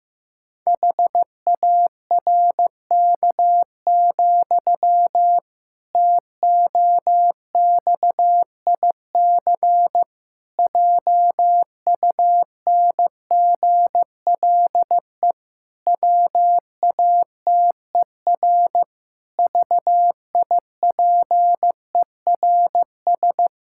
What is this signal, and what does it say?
Morse code, heard as a keyed tone, about 16 wpm, 700 Hz. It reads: HARK, TOXIC JUNGLE WATER VIPERS
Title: Pangram Challenge v2 - 15wpm